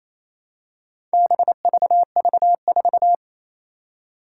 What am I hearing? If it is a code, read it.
Morse code, heard as a keyed tone, about 28 wpm, 700 Hz. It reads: BVV4
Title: Four Challenging Letter-Number Combinations 28wpm